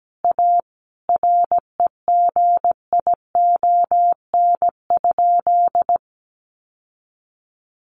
Morse code, heard as a keyed tone, about 17 wpm, 700 Hz. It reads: A REGION?
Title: Sentences from Top 1500 Words 17wpm